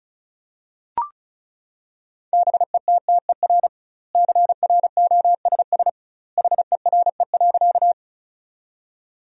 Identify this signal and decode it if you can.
Morse code, heard as a keyed tone, about 36 wpm, 700 Hz. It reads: BETTER CROSS HERE.
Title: 2x Word-Spacing - Sentences from Top 400 Words - 35wpm